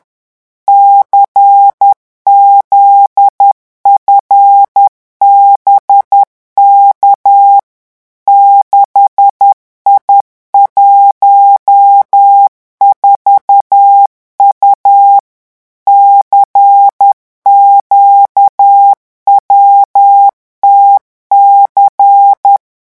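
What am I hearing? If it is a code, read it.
Morse code, heard as a keyed tone, about 10 wpm, 800 Hz. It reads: CZFBK 6I14U CQWTC
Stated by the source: Morse revision exercise for French radio amateurs at 10 words per minute.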